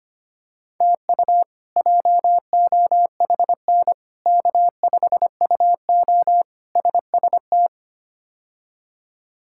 Morse code, heard as a keyed tone, about 25 wpm, 700 Hz. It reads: TU JOHN K5UO SST
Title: K1USN SST QSO Practice - 25wpm